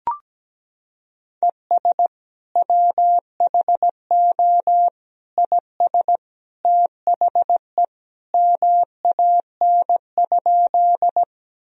Morse code, heard as a keyed tone, about 17 wpm, 700 Hz. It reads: ES WHO IS THE MAN?
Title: Sentences from Top 200 Words and Abbreviations 17wpm